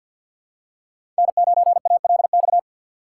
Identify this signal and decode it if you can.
Morse code, heard as a keyed tone, about 47 wpm, 700 Hz. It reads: N9ALX
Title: USA Call Signs - 5 Characters - 2x Character Spacing - 50wpm